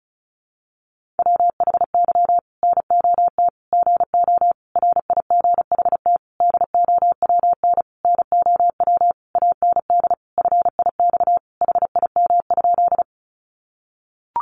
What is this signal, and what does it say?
Morse code, heard as a keyed tone, about 34 wpm, 700 Hz. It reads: WHY NOT GO RIGHT DOWN NOW AND FIX HIM?
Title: Sentences from Top 1400 Words 35wpm